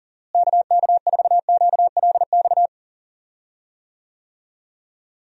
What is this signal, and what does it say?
Morse code, heard as a keyed tone, about 40 wpm, 700 Hz. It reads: KK4QLX
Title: USA Call Signs - 6 Characters - Repeated 3x - 40wpm